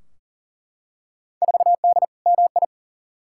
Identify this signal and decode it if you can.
Morse code, heard as a keyed tone, about 40 wpm, 700 Hz. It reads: 4D MI
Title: Field Day Exchanges Encore - 40wpm